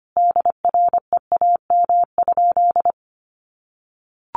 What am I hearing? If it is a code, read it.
Morse code, heard as a keyed tone, about 25 wpm, 700 Hz. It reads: DREAM?
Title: Sentences from Top 1500 Words 25wpm